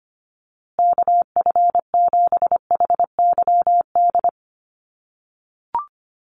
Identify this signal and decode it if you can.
Morse code, heard as a keyed tone, about 25 wpm, 700 Hz. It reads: KF7HYD
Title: USA Call Signs Repeated 2x 25wpm